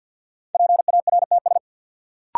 Morse code, heard as a keyed tone, about 52 wpm, 700 Hz. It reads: PARTS
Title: Sets of 3 Words - New Word Drill - 50wpm